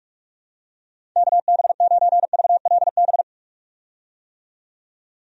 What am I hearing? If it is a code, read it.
Morse code, heard as a keyed tone, about 46 wpm, 700 Hz. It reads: KB9VLB
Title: USA Call Signs - 6 Characters - 45wpm